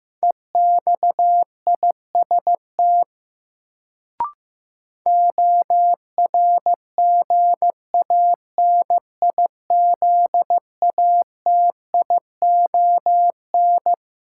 Morse code, heard as a keyed tone, about 16 wpm, 700 Hz. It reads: EXIST ORGANIZATION
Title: Sentences from Top 1000 Words - Review Part 5 - New Word Drill - 15wpm